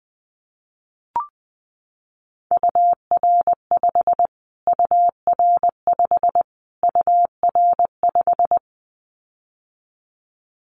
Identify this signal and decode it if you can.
Morse code, heard as a keyed tone, about 20 wpm, 700 Hz. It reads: UR5 UR5 UR5
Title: Callsign Prefixes for Top 25 DXCC - Repeated 3x - 20wpm